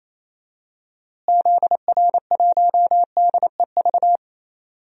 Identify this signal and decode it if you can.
Morse code, heard as a keyed tone, about 28 wpm, 700 Hz. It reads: ZR1DEV
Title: DXCC-100 Call Signs - 6 Characters - Repeated 3x - 28wpm